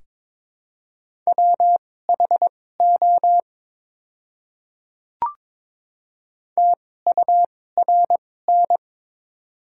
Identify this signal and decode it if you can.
Morse code, heard as a keyed tone, about 22 wpm, spaced as for 16 wpm, 700 Hz. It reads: WHO TURN
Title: 2x Character Spacing - Top 200 Words - 22wpm